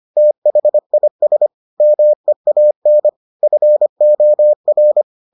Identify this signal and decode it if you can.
Morse code, heard as a keyed tone, about 25 wpm, 600 Hz. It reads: THIS MEAN FOR